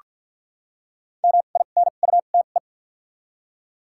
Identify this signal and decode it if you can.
Morse code, heard as a keyed tone, about 53 wpm, 700 Hz. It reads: M I N U T E
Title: 2x Character Spacing - Top 500 Words - 50wpm